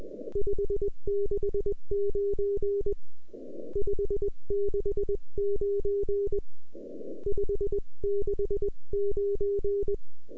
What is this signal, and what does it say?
Morse code, heard as a keyed tone, about 20 wpm, 400 Hz. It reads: 569 569 569